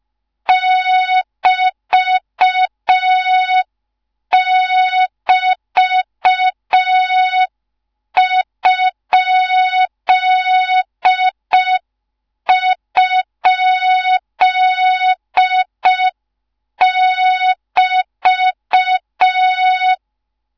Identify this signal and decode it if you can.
Morse code, heard as a keyed tone, about 5 wpm, 3700 Hz. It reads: <BT><BT>??<BT>